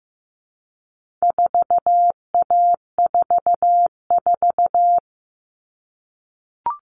Morse code, heard as a keyed tone, about 15 wpm, 700 Hz. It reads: 4A44
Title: All Cut Numbers - 4-Digit 15wpm